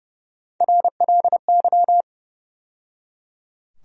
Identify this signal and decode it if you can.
Morse code, heard as a keyed tone, about 30 wpm, 700 Hz. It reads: RLY
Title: USA Call Sign Postfixes - 3 Letters 30wpm